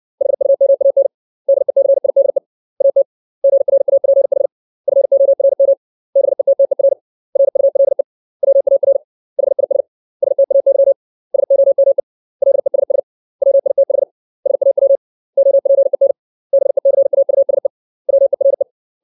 Morse code, heard as a keyed tone, about 60 wpm, 550 Hz. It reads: HUMAN DECIDE AT GRAPH FORM BETTER RULE WAR HIS STAY SOME LESS WITH SAW OPEN BECAUSE WERE
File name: 632 words / 632 Words 60WPM